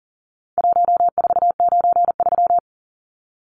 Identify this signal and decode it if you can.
Morse code, heard as a keyed tone, about 40 wpm, 700 Hz. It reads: 1493
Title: Numbers - Contest Style - 4-Digit 40wpm